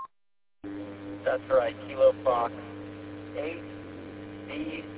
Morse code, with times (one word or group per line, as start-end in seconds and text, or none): none